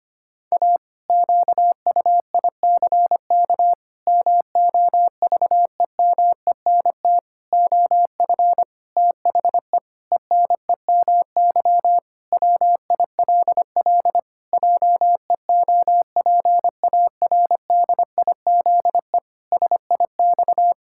Morse code, heard as a keyed tone, about 25 wpm, 700 Hz. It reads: A QUICK MOVEMENT OF THE ENEMY WILL JEOPARDIZE SIX
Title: Pangram Challenge v2 - 25wpm